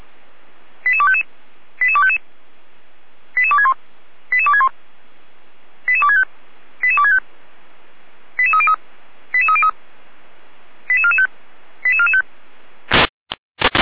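Five-tone selective call (ZVEI), heard as quick runs of five tones, tones of about 70 ms. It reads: 88255 88255 88261 88261 88265 88265 88303 88303 88445 88445
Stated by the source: ZVEI five-tone selective call received by SDR.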